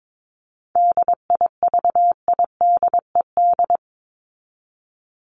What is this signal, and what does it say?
Morse code, heard as a keyed tone, about 22 wpm, 700 Hz. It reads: DIVIDED
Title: ED Suffix - Top 100 Words with Base Words 22wpm